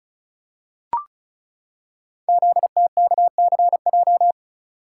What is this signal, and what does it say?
Morse code, heard as a keyed tone, about 36 wpm, 700 Hz. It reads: ZTKCJ
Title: Code Groups - 5 Letters - 35wpm